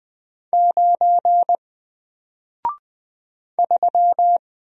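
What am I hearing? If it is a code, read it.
Morse code, heard as a keyed tone, about 20 wpm, 700 Hz. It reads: 9 3
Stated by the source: Single letter and number Morse drill at 20 wpm, from N4LCD.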